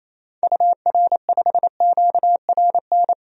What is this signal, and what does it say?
Morse code, heard as a keyed tone, about 28 wpm, 700 Hz. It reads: UR5QRN
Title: DXCC-100 Call Signs - 6 Characters - 28wpm